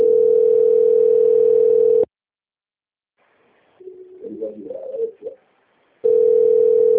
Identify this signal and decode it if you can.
Ringback tone (North America), 440 + 480 Hz.